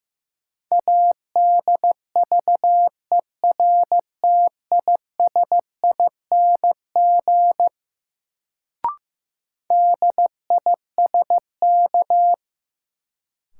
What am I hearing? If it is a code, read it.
Morse code, heard as a keyed tone, about 15 wpm, 700 Hz. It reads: ADVERTISING DISK
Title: Sentences from Top 1000 Words - Review Part 6 - New Word Drill - 15wpm